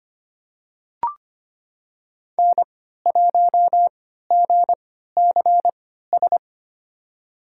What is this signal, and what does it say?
Morse code, heard as a keyed tone, about 25 wpm, spaced as for 14 wpm, 700 Hz. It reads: N1GCS
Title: USA Call Signs - 3x Character Spacing - 25wpm